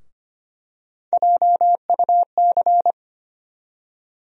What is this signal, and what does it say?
Morse code, heard as a keyed tone, about 25 wpm, 700 Hz. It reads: JUC
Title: USA Call Sign Postfixes - 3 Letters 25wpm